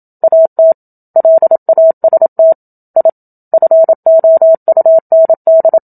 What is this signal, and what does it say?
Morse code, heard as a keyed tone, about 27 wpm, 650 Hz. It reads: AT LAST I FOUND
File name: Book Two / 01-Chapter 1 - 27 WPM